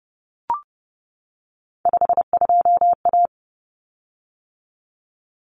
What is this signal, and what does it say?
Morse code, heard as a keyed tone, about 30 wpm, 700 Hz. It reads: 52A
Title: Common Cut Numbers - 3-Digit 30wpm